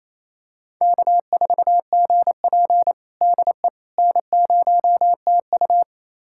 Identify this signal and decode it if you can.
Morse code, heard as a keyed tone, about 29 wpm, 700 Hz. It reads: K4GP DE N0TU
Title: USA Callsign DE Callsign - 4 Characters - 28wpm